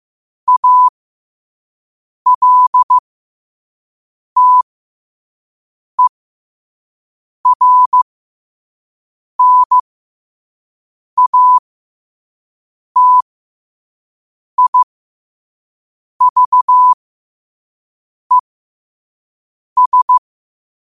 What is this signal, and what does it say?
Morse code, heard as a keyed tone, about 15 wpm, spaced as for 5 wpm, 1000 Hz. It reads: ALTERNATIVES
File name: qotdfiles / 5wpm